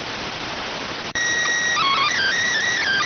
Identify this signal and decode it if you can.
SSTV (slow-scan television), PD 290.